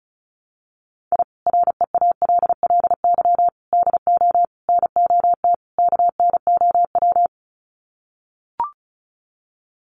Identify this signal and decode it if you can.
Morse code, heard as a keyed tone, about 34 wpm, 700 Hz. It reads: I REALLY DO NOT KNOW
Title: Sets of 5 Words Encore 35wpm